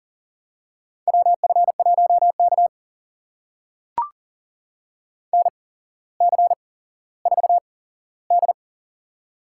Morse code, heard as a keyed tone, about 40 wpm, spaced as for 17 wpm, 700 Hz. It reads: WF1K NC4D
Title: USA Call Signs - 4 Characters - 8x Character Spacing - 40wpm